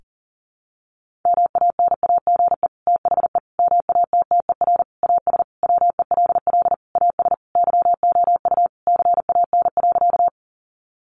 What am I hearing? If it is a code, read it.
Morse code, heard as a keyed tone, about 40 wpm, 700 Hz. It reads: MANAGE THE MATTER AS WELL AS YOU CAN.
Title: Sentences from Top 1400 Words 40wpm